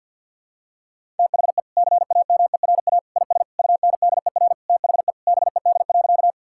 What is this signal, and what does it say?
Morse code, heard as a keyed tone, about 52 wpm, 700 Hz. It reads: THE CAMERA IS UNDER THE BED.
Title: Sentences from Top 1000 Words - Review Part 5 - 50wpm